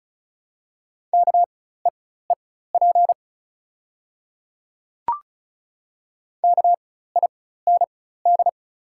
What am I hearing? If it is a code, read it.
Morse code, heard as a keyed tone, about 36 wpm, spaced as for 16 wpm, 690 Hz. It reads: KEEP KIND
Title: 4x Character Spacing - Top 500 Words - 35wpm